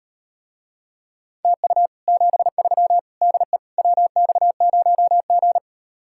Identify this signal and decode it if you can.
Morse code, heard as a keyed tone, about 38 wpm, 700 Hz. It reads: TU 73 DE WX0G